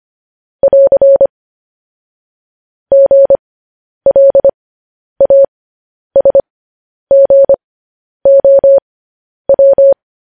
Morse code, heard as a keyed tone, about 25 wpm, spaced as for 10 wpm, 550 Hz. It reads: <AR> GLASGOW
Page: news headlines in morse code at 10 wpm